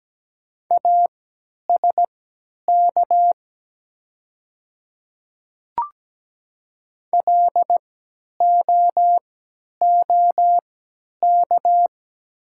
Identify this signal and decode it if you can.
Morse code, heard as a keyed tone, about 17 wpm, spaced as for 10 wpm, 690 Hz. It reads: ASK LOOK
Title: 3x Character Spacing - Top 300 Words - 17wpm